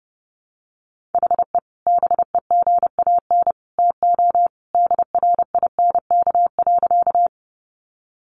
Morse code, heard as a keyed tone, about 30 wpm, 700 Hz. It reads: HE BEGAN TO DRINK.
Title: Sentences from Top 1500 Words 30wpm